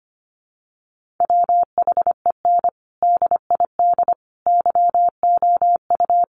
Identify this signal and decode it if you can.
Morse code, heard as a keyed tone, about 25 wpm, 700 Hz. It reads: WHEN DID YOU
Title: Sentences from Top 300 Words 25wpm